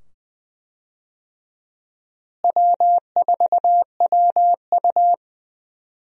Morse code, heard as a keyed tone, about 20 wpm, 700 Hz. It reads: W4WU
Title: USA Call Signs - 4 Characters - Repeated 2x - 20wpm